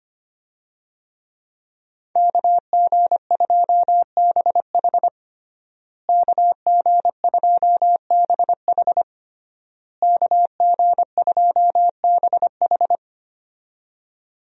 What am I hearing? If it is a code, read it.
Morse code, heard as a keyed tone, about 25 wpm, 700 Hz. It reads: KG2BH KG2BH KG2BH
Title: USA Call Signs - 5 Characters - Repeated 3x - 25wpm